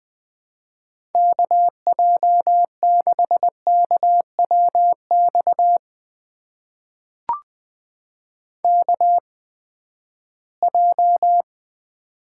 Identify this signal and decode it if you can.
Morse code, heard as a keyed tone, about 20 wpm, spaced as for 14 wpm, 700 Hz. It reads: KJ6KWX KJ
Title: USA Call Signs - 6 Characters - 8x Character Spacing - 20wpm